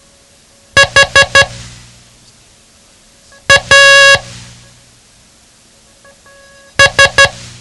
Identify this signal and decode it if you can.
Morse code, heard as a keyed tone, about 12 wpm, spaced as for 3 wpm, 1650 Hz. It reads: HAS
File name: ham radio / CodeQuick / Code Quick 07